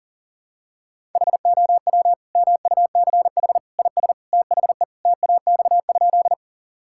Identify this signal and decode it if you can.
Morse code, heard as a keyed tone, about 40 wpm, 700 Hz. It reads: HOW MUCH IS THE TAX?
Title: Sentences from Top 1000 Words - Review Part 5 - 40wpm